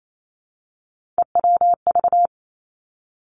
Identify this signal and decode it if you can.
Morse code, heard as a keyed tone, about 28 wpm, 700 Hz. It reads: EWV
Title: Three Challenging Letter-Number Combinations 28wpm